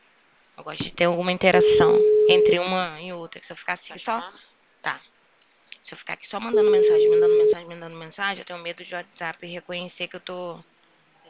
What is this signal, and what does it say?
Ringback tone (Europe, 425 Hz), with speech.